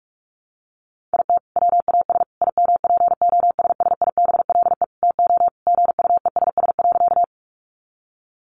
Morse code, heard as a keyed tone, about 44 wpm, 700 Hz. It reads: IT WAS IMPOSSIBLE TO GUESS.
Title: Sentences from Top 1500 Words 45wpm